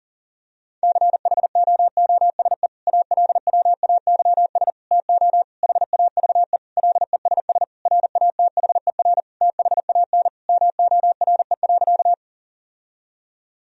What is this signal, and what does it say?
Morse code, heard as a keyed tone, about 37 wpm, 700 Hz. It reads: CHOOSE ALWAYS TO HAVE LESS RATHER THAN MORE.
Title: Sentences from Top 1000 Words - Review Part 4 - 40wpm